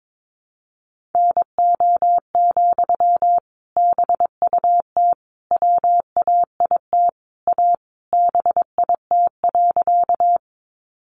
Morse code, heard as a keyed tone, about 22 wpm, 700 Hz. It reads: NO, BUT WAIT A BIT.
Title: Sentences from Top 900 Words 22wpm